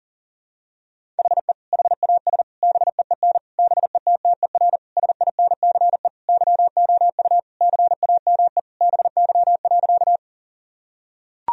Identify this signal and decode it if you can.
Morse code, heard as a keyed tone, about 40 wpm, 700 Hz. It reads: HE HAS BEEN BETTER SINCE YOU CAME BY.
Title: Sentences from Top 1000 Words - Review Part 1 - 40wpm